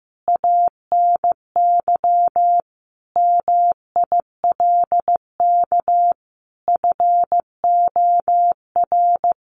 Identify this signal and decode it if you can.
Morse code, heard as a keyed tone, about 15 wpm, 700 Hz. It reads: ANY MILK FOR
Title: Sentences from Top 1500 Words 15wpm